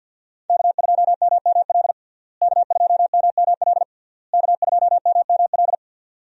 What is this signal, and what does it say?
Morse code, heard as a keyed tone, about 49 wpm, 700 Hz. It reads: K2MML K2MML K2MML
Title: USA Call Signs - 5 Characters - Repeated 3x - 50wpm